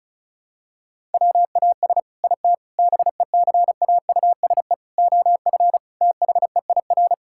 Morse code, heard as a keyed tone, about 36 wpm, 700 Hz. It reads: WAS IT BECAUSE OF THEIR
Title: Sentences from Top 1000 Words - Review Part 6 - 35wpm